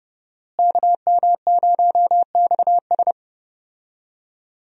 Morse code, heard as a keyed tone, about 32 wpm, 700 Hz. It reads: KM0XS
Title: USA Call Signs - 5 Characters - Repeated 2x - 30wpm